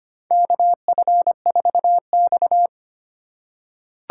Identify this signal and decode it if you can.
Morse code, heard as a keyed tone, about 25 wpm, 700 Hz. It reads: KF4X
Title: USA Call Signs - 4 Characters - 2x Character Spacing - 25wpm